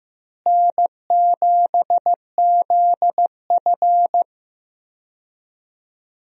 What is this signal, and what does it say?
Morse code, heard as a keyed tone, about 15 wpm, 700 Hz. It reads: N7ZF
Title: USA Call Signs - 4 Characters - Repeated 2x - 15wpm